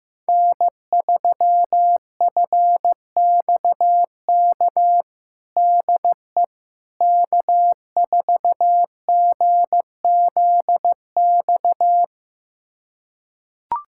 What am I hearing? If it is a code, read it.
Morse code, heard as a keyed tone, about 15 wpm, 700 Hz. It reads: N3FXK DE K4GZX
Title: USA Callsign DE Callsign - 5 Characters - 15wpm